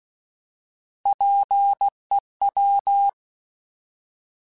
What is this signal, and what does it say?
Morse code, heard as a keyed tone, about 16 wpm, 800 Hz. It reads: PEW
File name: Alphabetical / Alphabetical mp3s / Ps @15